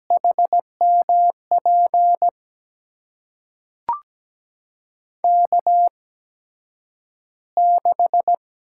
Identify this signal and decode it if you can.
Morse code, heard as a keyed tone, about 17 wpm, spaced as for 9 wpm, 700 Hz. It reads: HMP K6
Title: USA Call Signs - 8x Character Spacing - 17wpm